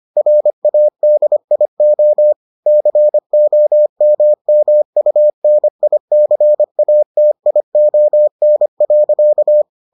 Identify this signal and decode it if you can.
Morse code, heard as a keyed tone, about 25 wpm, 600 Hz. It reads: RADIO COMMUNICATION.